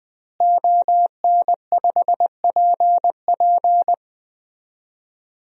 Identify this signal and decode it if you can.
Morse code, heard as a keyed tone, about 20 wpm, 700 Hz. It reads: ON5PP
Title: DXCC-100 Call Signs - 5 Characters - Repeated 2x - 20wpm